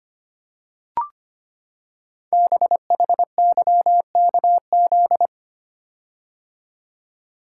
Morse code, heard as a keyed tone, about 25 wpm, 700 Hz. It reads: BHYKZ
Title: Code Groups - 5 Letters - 25wpm